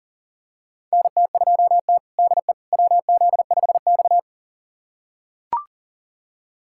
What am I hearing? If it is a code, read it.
Morse code, heard as a keyed tone, about 40 wpm, 700 Hz. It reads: NT2T DE WZ5X
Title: USA Callsign DE Callsign - 4 Characters - 40wpm